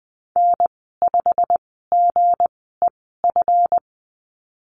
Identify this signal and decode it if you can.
Morse code, heard as a keyed tone, about 20 wpm, spaced as for 15 wpm, 700 Hz. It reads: N5GEF
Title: 161 Lesson 26g - Letter g - Call Signs 20wpm